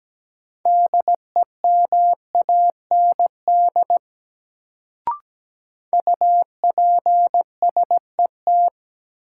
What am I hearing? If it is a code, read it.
Morse code, heard as a keyed tone, about 17 wpm, 700 Hz. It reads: DEMAND UPSET
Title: Sentences from Top 800 Words - New Word Drill - 17wpm